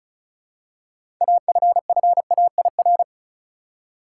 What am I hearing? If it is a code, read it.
Morse code, heard as a keyed tone, about 36 wpm, 700 Hz. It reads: AFFAIR
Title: Six Letter Words - 35wpm